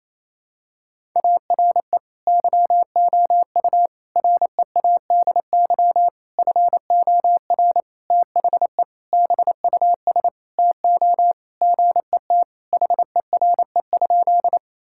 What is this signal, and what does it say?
Morse code, heard as a keyed tone, about 28 wpm, 700 Hz. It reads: ARE YOU READY FOR THE BUS TO GET HERE?
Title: Sentences from Top 1000 Words - Review Part 6 - 28wpm